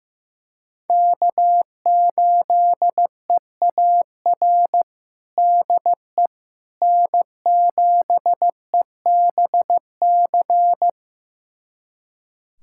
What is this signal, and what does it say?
Morse code, heard as a keyed tone, about 15 wpm, 700 Hz. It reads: K8EAR DE N7EBC
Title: USA Callsign DE Callsign - 5 Characters - 15wpm